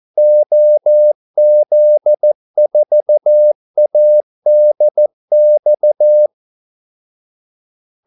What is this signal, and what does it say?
Morse code, heard as a keyed tone, about 14 wpm, 600 Hz. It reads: OZ4ADX
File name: Callsigns-14wpm